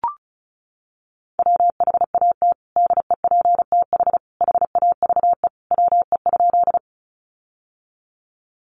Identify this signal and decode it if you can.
Morse code, heard as a keyed tone, about 34 wpm, 700 Hz. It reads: WHAT DEPTH HAVE WE?
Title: Sentences from Top 1200 Words 35wpm